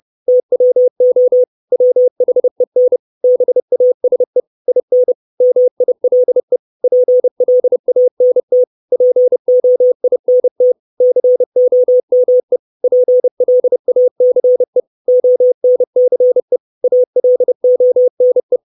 Morse code, heard as a keyed tone, about 30 wpm, 500 Hz. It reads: TWO WHEN BASE IN MILE PLANT POINT COME PLACE ONCE ALONE
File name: 632 Words 30WPM